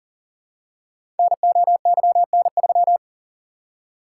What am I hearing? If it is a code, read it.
Morse code, heard as a keyed tone, about 42 wpm, 690 Hz. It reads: NOYN3
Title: Code Groups - 5 Letter:Numbers - 40wpm